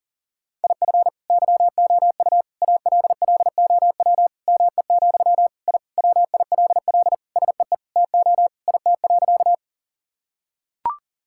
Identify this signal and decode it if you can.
Morse code, heard as a keyed tone, about 37 wpm, 700 Hz. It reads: IF YOU ALLOW ME, I WILL SEE TO IT.
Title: Sentences from Top 1000 Words - Review Part 5 - 40wpm